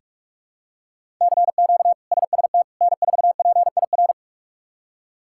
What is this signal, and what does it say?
Morse code, heard as a keyed tone, about 46 wpm, 700 Hz. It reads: CQ SST N4PIR
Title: K1USN SST QSO Practice - 45wpm